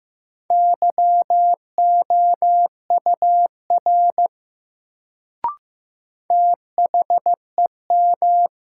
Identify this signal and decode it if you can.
Morse code, heard as a keyed tone, about 15 wpm, 700 Hz. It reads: YOUR THEM
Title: Sentences from Top 100 Words - New Word Drill - 15wpm